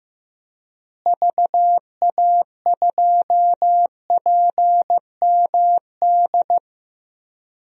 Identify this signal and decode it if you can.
Morse code, heard as a keyed tone, about 15 wpm, 700 Hz. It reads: VA2PMD
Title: DXCC-100 Call Signs - 6 Characters - Repeated 3x - 15wpm